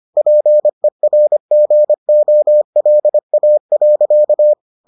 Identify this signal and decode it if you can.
Morse code, heard as a keyed tone, about 25 wpm, 600 Hz. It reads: PERGOLA.